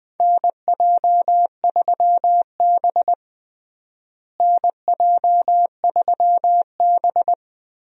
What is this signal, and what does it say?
Morse code, heard as a keyed tone, about 20 wpm, 700 Hz. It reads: NJ3B NJ3B
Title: USA Call Signs - 4 Characters - Repeated 3x - 20wpm